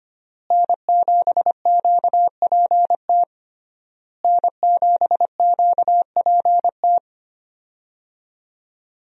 Morse code, heard as a keyed tone, about 25 wpm, 700 Hz. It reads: N7QPT N7QPT
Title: USA Call Signs - 5 Characters - Repeated 3x - 25wpm